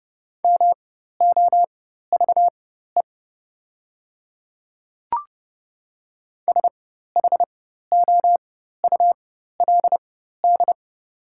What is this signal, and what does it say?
Morse code, heard as a keyed tone, about 30 wpm, spaced as for 14 wpm, 700 Hz. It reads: MOVE SHOULD
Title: 4x Character Spacing - Top 300 Words - 30wpm